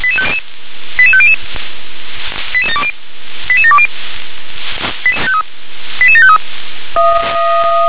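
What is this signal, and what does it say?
Five-tone selective call (ZVEI), heard as quick runs of five tones, tones of about 70 ms. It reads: 88500 88720 88753
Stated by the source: ZVEI five-tone selective call received on SDR.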